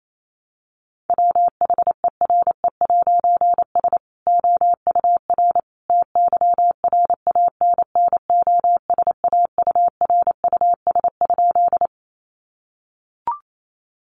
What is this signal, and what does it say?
Morse code, heard as a keyed tone, about 28 wpm, 700 Hz. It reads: WHERE'S OUR TYRANNOSAURUS?
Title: A Sound Of Thunder 28wpm